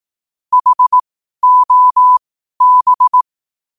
Morse code, heard as a keyed tone, about 18 wpm, spaced as for 12 wpm, 1000 Hz. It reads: HOB
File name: qotdfiles / 12wpm